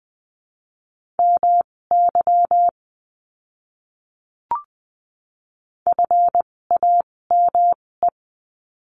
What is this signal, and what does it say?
Morse code, heard as a keyed tone, about 20 wpm, spaced as for 16 wpm, 700 Hz. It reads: MY FAME
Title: 256 Lesson 42a - Words 12-20wpm